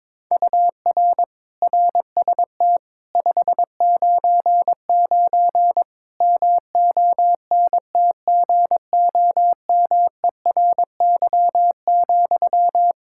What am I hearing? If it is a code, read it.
Morse code, heard as a keyed tone, about 21 wpm, 700 Hz. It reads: UR RST 599 MONTGOMERY,
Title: K1USN SST QSO Practice 22wpm